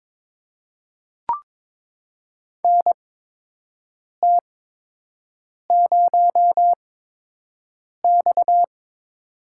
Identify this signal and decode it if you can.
Morse code, heard as a keyed tone, about 22 wpm, spaced as for 6 wpm, 700 Hz. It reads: NT0X
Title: USA Call Signs - 8x Character Spacing - 22wpm